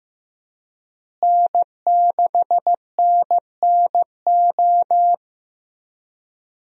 Morse code, heard as a keyed tone, about 15 wpm, 700 Hz. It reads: N6NNO
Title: USA Call Signs - 5 Characters - Repeated 3x - 15wpm